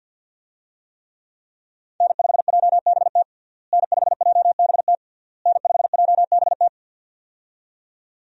Morse code, heard as a keyed tone, about 53 wpm, 700 Hz. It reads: N5JBT N5JBT N5JBT
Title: USA Call Signs - 5 Characters - Repeated 3x - 50wpm